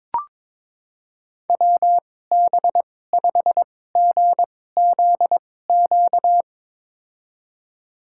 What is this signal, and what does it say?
Morse code, heard as a keyed tone, about 22 wpm, spaced as for 16 wpm, 700 Hz. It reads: WB5GZQ
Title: USA Call Signs - 6 Characters - 2x Character Spacing - 22wpm